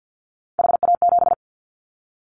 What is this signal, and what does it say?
Morse code, heard as a keyed tone, about 50 wpm, 700 Hz. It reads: HA7
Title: Callsign Prefixes for Top 25 DXCC - 50wpm